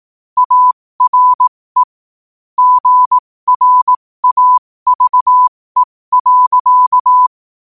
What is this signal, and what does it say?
Morse code, heard as a keyed tone, about 18 wpm, spaced as for 15 wpm, 1000 Hz. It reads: ARE GRAVE.